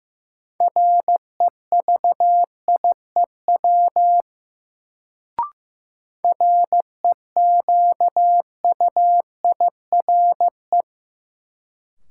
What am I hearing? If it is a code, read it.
Morse code, heard as a keyed tone, about 16 wpm, 700 Hz. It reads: REVIEW REQUIRE
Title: Sentences from Top 900 Words - New Word Drill - 15wpm